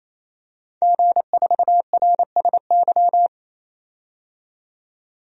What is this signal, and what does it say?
Morse code, heard as a keyed tone, about 27 wpm, 690 Hz. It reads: G4RSY
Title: DXCC-100 Call Signs - 5 Characters - 28wpm